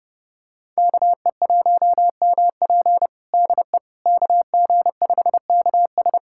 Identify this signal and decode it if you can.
Morse code, heard as a keyed tone, about 32 wpm, 700 Hz. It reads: KE1MP DE KG5KS
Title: USA Callsign DE Callsign - 5 Characters - 30wpm